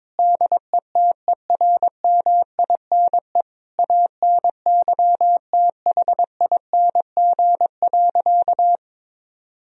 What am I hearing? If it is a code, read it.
Morse code, heard as a keyed tone, about 22 wpm, 700 Hz. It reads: DETERMINE ANYTHING.
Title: Sentences from Top 1000 Words - Review Part 4 - 22wpm